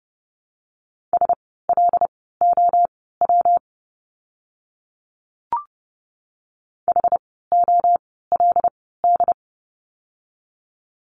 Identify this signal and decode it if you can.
Morse code, heard as a keyed tone, about 30 wpm, spaced as for 17 wpm, 700 Hz. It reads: SLOW HOLD
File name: course / 30:10wpm / 115 Lesson 19f - Letter w - Words 30wpm